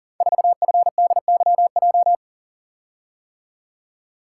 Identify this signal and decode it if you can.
Morse code, heard as a keyed tone, about 38 wpm, 700 Hz. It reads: 4FDYJ
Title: Code Groups - 5 Letter:Numbers - 40wpm